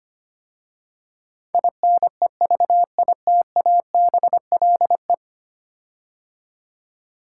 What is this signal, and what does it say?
Morse code, heard as a keyed tone, about 25 wpm, 700 Hz. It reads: INEVITABLE
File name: 10 Letter Words - First 3 Letters Given - 25wpm